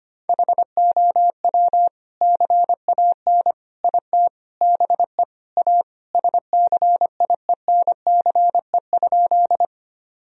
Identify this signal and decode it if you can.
Morse code, heard as a keyed tone, about 25 wpm, 700 Hz. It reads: HOW CAN IT BE A SCIENCE?
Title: Sentences from Top 1000 Words - Review Part 2 - 25wpm